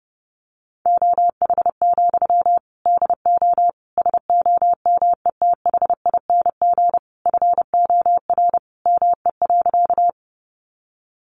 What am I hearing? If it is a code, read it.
Morse code, heard as a keyed tone, about 30 wpm, 700 Hz. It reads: OH, DO SOMETHING FOR ME.
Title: Sentences from Top 600 Words 30wpm